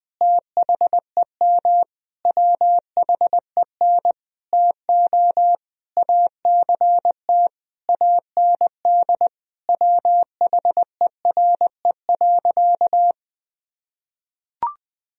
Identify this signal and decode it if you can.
Morse code, heard as a keyed tone, about 20 wpm, 700 Hz. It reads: THEM WHEN TO ACT AND WHERE.
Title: Sentences from Top 1000 Words - Review Part 1 - 20wpm